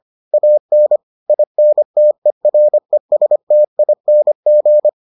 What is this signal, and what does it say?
Morse code, heard as a keyed tone, about 25 wpm, 600 Hz. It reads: AN INTERESTING